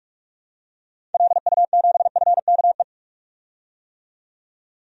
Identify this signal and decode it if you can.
Morse code, heard as a keyed tone, about 46 wpm, 700 Hz. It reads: LU7FKE
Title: DXCC-100 Call Signs - 6 Characters - 45wpm